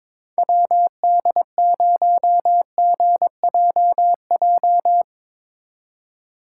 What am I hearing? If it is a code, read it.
Morse code, heard as a keyed tone, about 22 wpm, 700 Hz. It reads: WD0GJJ
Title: Speed Racing - USA 2x3 Call Signs 22wpm